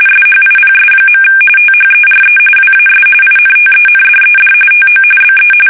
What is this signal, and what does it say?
RTTY, about 45.45 baud. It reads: SHIPS. THERE WAS INTENDED TO BE
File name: fsk850 / battleships